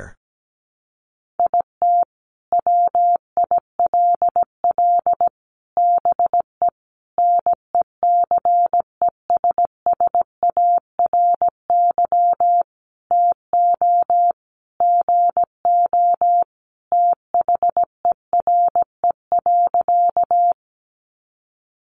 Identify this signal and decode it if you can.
Morse code, heard as a keyed tone, about 17 wpm, 700 Hz. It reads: IT WILL BE NECESSARY TO GO THERE.
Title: Sentences from Top 800 Words 17wpm